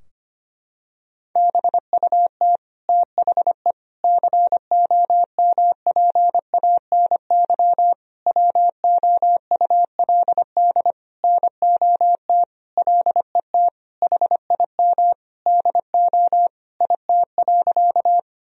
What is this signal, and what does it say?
Morse code, heard as a keyed tone, about 25 wpm, 700 Hz. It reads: BUT THE COMPANY WOULD NOT LET HIM DO IT.
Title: Sentences from Top 1000 Words - Review Part 3 - 25wpm